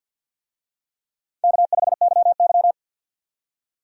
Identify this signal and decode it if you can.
Morse code, heard as a keyed tone, about 51 wpm, 700 Hz. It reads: K5YY
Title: USA Call Signs - 4 Characters - 50wpm